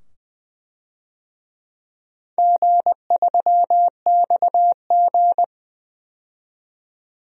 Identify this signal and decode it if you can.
Morse code, heard as a keyed tone, about 20 wpm, 700 Hz. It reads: G3XG
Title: DXCC-100 Call Signs - 4 Characters - 20wpm